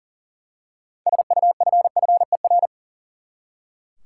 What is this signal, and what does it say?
Morse code, heard as a keyed tone, about 40 wpm, 700 Hz. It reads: SUFFER
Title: Six Letter Words - 40wpm